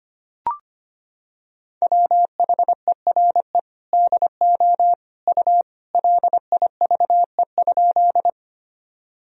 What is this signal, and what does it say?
Morse code, heard as a keyed tone, about 25 wpm, 700 Hz. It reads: WHERE DO U LIVE?
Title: Sentences from Top 200 Words and Abbreviations 25wpm